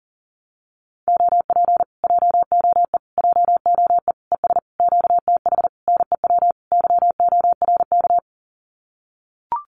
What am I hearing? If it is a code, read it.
Morse code, heard as a keyed tone, about 40 wpm, 700 Hz. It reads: OP JOE JOE ES QTH NEW YORK
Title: Common QSO Phrases Encore 40wpm